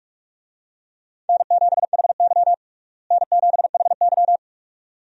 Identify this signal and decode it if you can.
Morse code, heard as a keyed tone, about 42 wpm, 700 Hz. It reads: N7HY N7HY
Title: USA Call Signs - 4 Characters - Repeated 2x - 45wpm